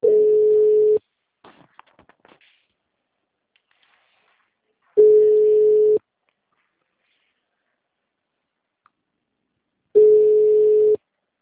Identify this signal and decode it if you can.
Ringback tone (Europe, 425 Hz).